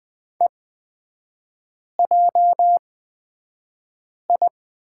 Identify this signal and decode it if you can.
Morse code, heard as a keyed tone, about 20 wpm, spaced as for 5 wpm, 700 Hz. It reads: EJI
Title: 255 Lesson 41e - Period - Three Characters 20wpm